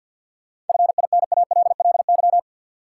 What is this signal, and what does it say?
Morse code, heard as a keyed tone, about 52 wpm, 700 Hz. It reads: FINALLY
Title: Sentences from Top 1000 Words - New Word Drill - 50wpm